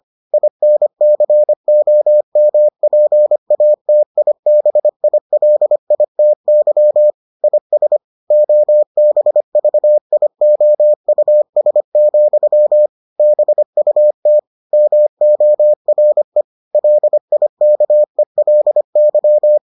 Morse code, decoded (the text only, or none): INCOMPATIBILITY IS OBVIOUS, BUT MORE LIKELY